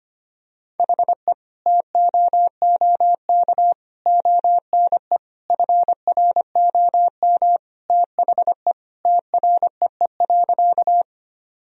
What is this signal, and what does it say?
Morse code, heard as a keyed tone, about 25 wpm, 700 Hz. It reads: HE TOOK ONE FROM THE TREE.
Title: Sentences from Top 1000 Words - Review Part 1 - 25wpm